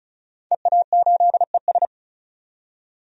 Morse code, heard as a keyed tone, about 35 wpm, 700 Hz. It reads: EA8ES
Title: DXCC-100 Call Signs - 5 Characters - 35wpm